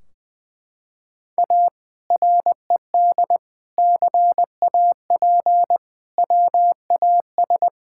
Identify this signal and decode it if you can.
Morse code, heard as a keyed tone, about 20 wpm, 700 Hz. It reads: A RED CAP WAS
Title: Sentences from Top 1600 Words 20wpm